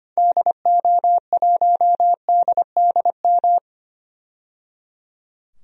Morse code, heard as a keyed tone, about 25 wpm, 700 Hz. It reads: DO1DDM
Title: DXCC-100 Call Signs - 6 Characters - 25wpm